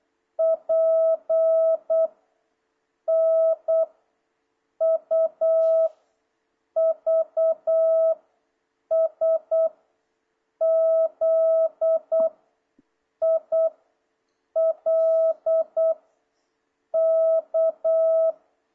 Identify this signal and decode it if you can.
Morse code, heard as a keyed tone, about 8 wpm, 660 Hz. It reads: P N U V S Z I L K
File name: jh8cbh / 2026cw